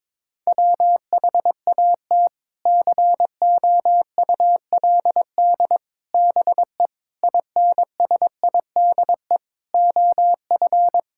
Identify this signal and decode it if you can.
Morse code, heard as a keyed tone, about 21 wpm, 700 Hz. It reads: WHAT COULD BE INSIDE OF
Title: Sentences from Top 1000 Words - Review Part 6 - 22wpm